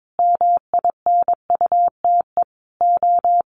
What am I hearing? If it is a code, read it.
Morse code, heard as a keyed tone, about 22 wpm, 700 Hz. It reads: MINUTE O
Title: Sentences from Top 1400 Words 22wpm